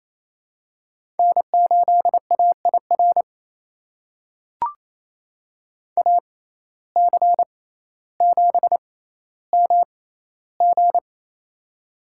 Morse code, heard as a keyed tone, about 28 wpm, spaced as for 14 wpm, 700 Hz. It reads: N8AIR AC7MG
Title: USA Call Signs - 5 Characters - 6x Character Spacing - 28wpm